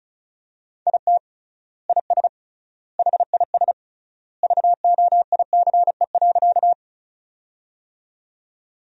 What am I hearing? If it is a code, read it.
Morse code, heard as a keyed tone, about 36 wpm, 700 Hz. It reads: IT IS HIS VOICE.
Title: 3x Word-Spacing - Sentences from Top 500 Words - 35wpm